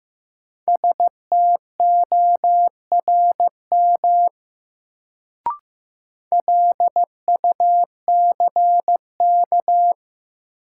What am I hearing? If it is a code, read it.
Morse code, heard as a keyed tone, about 15 wpm, 700 Hz. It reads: STORM LUCK
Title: Sentences from Top 1500 Words - New Word Drill Alt - 15wpm